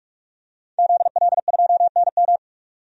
Morse code, heard as a keyed tone, about 46 wpm, 700 Hz. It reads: ZL2NM
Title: DXCC-100 Call Signs - 5 Characters - 45wpm